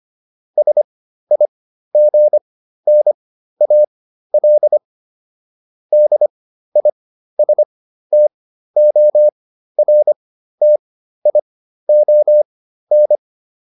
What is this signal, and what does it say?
Morse code, heard as a keyed tone, about 25 wpm, spaced as for 13 wpm, 600 Hz. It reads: SIGNAL DISTORTION